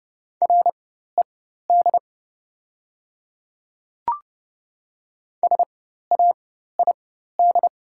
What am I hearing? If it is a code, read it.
Morse code, heard as a keyed tone, about 30 wpm, spaced as for 14 wpm, 700 Hz. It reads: RED SAID
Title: 4x Character Spacing - Top 500 Words - 30wpm